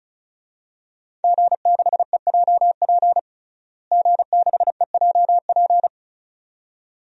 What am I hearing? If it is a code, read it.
Morse code, heard as a keyed tone, about 34 wpm, 700 Hz. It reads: G6EJP G6EJP
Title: DXCC-100 Call Signs - 5 Characters - Repeated 2x - 35wpm